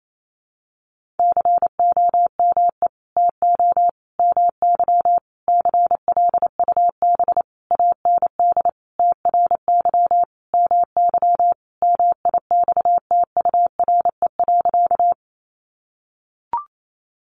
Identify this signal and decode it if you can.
Morse code, heard as a keyed tone, about 28 wpm, 700 Hz. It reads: COME TO MY CLUB AND TRY MY MIXTURE.
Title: Sentences from Top 1500 Words 28wpm